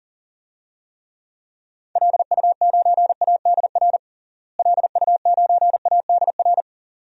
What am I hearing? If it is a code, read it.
Morse code, heard as a keyed tone, about 40 wpm, 700 Hz. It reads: LU9ADR LU9ADR
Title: DXCC-100 Call Signs - 6 Characters - Repeated 2x - 40wpm